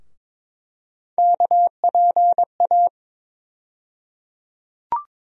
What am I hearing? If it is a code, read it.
Morse code, heard as a keyed tone, about 22 wpm, 700 Hz. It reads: KPA
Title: USA Call Sign Postfixes - 3 Letters 22wpm